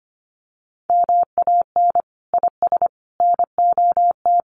Morse code, heard as a keyed tone, about 25 wpm, 700 Hz. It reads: MAN IS NOT
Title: Sentences from Top 1400 Words 25wpm